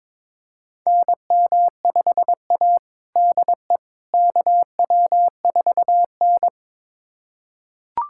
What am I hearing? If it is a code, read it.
Morse code, heard as a keyed tone, about 22 wpm, 700 Hz. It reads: NM5A DE KW4N
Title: USA Callsign DE Callsign - 4 Characters - 22wpm